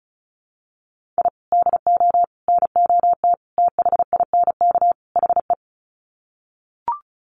Morse code, heard as a keyed tone, about 34 wpm, 700 Hz. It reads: I DO NOT THINK HE
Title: Sets of 5 Words 35wpm